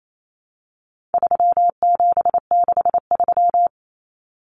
Morse code, heard as a keyed tone, about 28 wpm, 700 Hz. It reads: 3763
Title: Common Cut Numbers - 4-Digit 28wpm